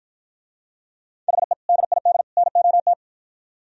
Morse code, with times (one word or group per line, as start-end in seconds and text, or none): 1.3-1.5 HE
1.7-2.2 DID
2.4-2.9 NOT